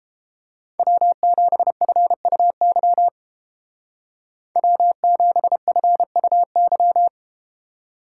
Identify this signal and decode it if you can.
Morse code, heard as a keyed tone, about 32 wpm, 700 Hz. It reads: W7FUY W7FUY